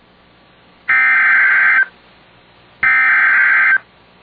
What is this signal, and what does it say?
Emergency Alert System (SAME) tones (header bursts), heard as harsh buzzing bursts. It is heard from 0.9 to 3.7 s.